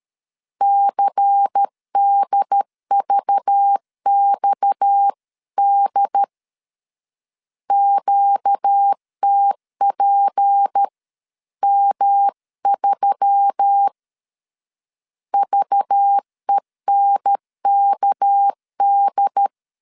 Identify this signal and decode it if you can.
Morse code, heard as a keyed tone, about 13 wpm, 800 Hz. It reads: CDVXD QTPM3 VENKD
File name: CW / DET / 1 6 1500C S B 12wpm 3431